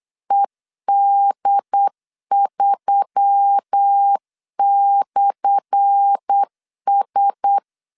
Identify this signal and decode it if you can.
Morse code, heard as a keyed tone, about 8 wpm, 800 Hz. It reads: ED3/S
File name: LEZIONI / 10 SIGNS / 10A SLASH 8wpm 3452